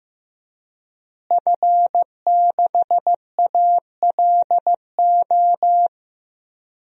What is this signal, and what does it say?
Morse code, heard as a keyed tone, about 15 wpm, 700 Hz. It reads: F6ALO
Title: DXCC-100 Call Signs - 5 Characters - Repeated 3x - 15wpm